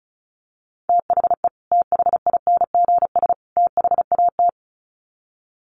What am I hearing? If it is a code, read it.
Morse code, heard as a keyed tone, about 34 wpm, 700 Hz. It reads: THE THINGS THAT
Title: Sets of 3 Words 35wpm